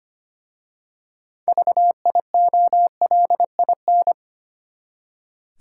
Morse code, heard as a keyed tone, about 25 wpm, 700 Hz. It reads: VIOLIN
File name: practice / 25wpm / Easy Words with Hints - 25wpm